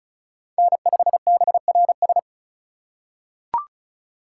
Morse code, heard as a keyed tone, about 34 wpm, 700 Hz. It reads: N5BRS
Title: USA Call Signs - 5 Characters - 3x Character Spacing - 35wpm